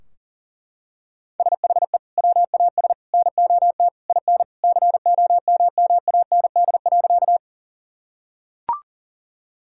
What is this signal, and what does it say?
Morse code, heard as a keyed tone, about 40 wpm, 700 Hz. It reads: SHE WAS NOT IN COMMAND.
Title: Sentences from Top 1600 Words 40wpm